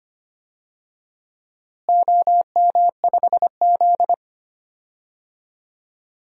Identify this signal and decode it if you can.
Morse code, heard as a keyed tone, about 25 wpm, 700 Hz. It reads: OM5Z
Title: DXCC-100 Call Signs - 4 Characters - Repeated 3x - 25wpm